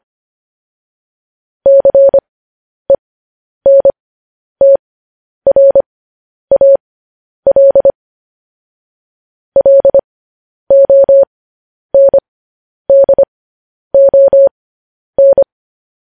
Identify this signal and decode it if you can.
Morse code, heard as a keyed tone, about 25 wpm, spaced as for 10 wpm, 550 Hz. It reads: CENTRAL LONDON